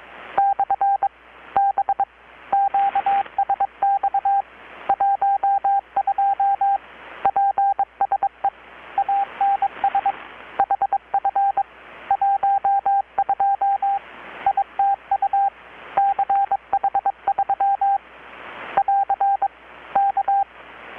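Morse code, heard as a keyed tone, about 22 wpm, 790 Hz, with radio noise. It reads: / B QSX 12 PSE ANS HF 12 ITU CH3 <AR> K
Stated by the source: Morse from KPH's Night of Nights XX maritime radiotelegraphy event, received in Chiapas, Mexico about 3,800 km away on a Kenwood R-600 with a grounded 10 m long-wire antenna.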